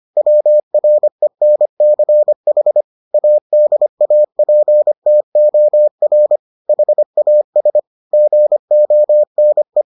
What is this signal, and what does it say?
Morse code, heard as a keyed tone, about 25 wpm, 600 Hz. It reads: WRENCH ADAPTOR HAS GONE